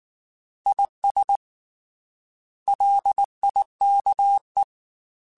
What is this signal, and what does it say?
Morse code, heard as a keyed tone, about 19 wpm, 800 Hz. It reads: IS LIKE
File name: Weather Sets / Weather Sets-mp3s / Weather 4 17 @18